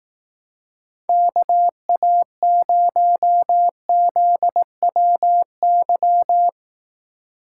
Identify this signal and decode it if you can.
Morse code, heard as a keyed tone, about 18 wpm, 700 Hz. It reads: KA0ZWY
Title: Speed Racing - USA Call Signs 15wpm